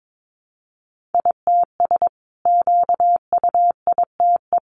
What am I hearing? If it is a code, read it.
Morse code, heard as a keyed tone, about 22 wpm, 700 Hz. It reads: ITS QUITE
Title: Sentences from Top 700 Words 22wpm